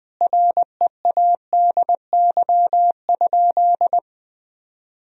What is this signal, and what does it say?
Morse code, heard as a keyed tone, about 21 wpm, 700 Hz. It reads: READY?
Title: Sentences from Top 400 Words and Abbreviations 20wpm